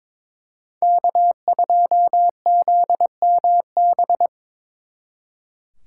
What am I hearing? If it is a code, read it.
Morse code, heard as a keyed tone, about 26 wpm, 700 Hz. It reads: K2ZMB
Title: Speed Racing - USA 1x3 Call Signs 22wpm